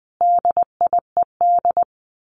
Morse code, heard as a keyed tone, about 20 wpm, 700 Hz. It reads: DIED
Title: A Sound Of Thunder 20wpm